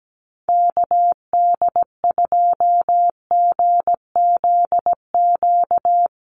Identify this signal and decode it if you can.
Morse code, heard as a keyed tone, about 17 wpm, 700 Hz. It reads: KD2GZQ
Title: USA Call Signs Repeated 3x 17wpm